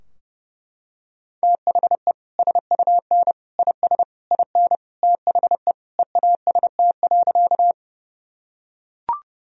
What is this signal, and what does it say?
Morse code, heard as a keyed tone, about 29 wpm, 700 Hz. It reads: THE SUN IS IN THE EAST.
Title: Sentences from Top 1000 Words - Review Part 3 - 30wpm